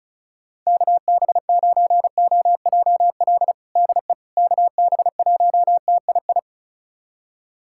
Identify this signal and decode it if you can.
Morse code, heard as a keyed tone, about 36 wpm, 700 Hz. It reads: KB9OJL DE KB1TII
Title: USA Callsign DE Callsign 35wpm